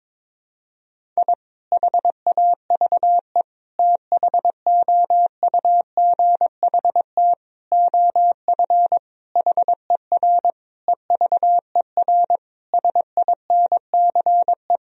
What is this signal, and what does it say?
Morse code, heard as a keyed tone, about 22 wpm, 700 Hz. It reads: I HAVE THOUGHT OF HER EVER SINCE
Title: Sentences from Top 1000 Words - Review Part 1 - 22wpm